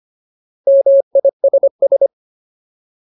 Morse code, heard as a keyed tone, about 25 wpm, 550 Hz. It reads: MISS
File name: Ultimate Word List / 25WPM Ultimate